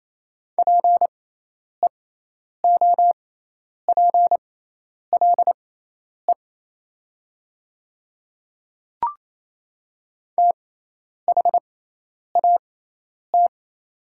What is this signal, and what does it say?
Morse code, heard as a keyed tone, about 27 wpm, spaced as for 10 wpm, 700 Hz. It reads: PEOPLE THAT
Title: 6x Character Spacing - Top 200 Words - 28wpm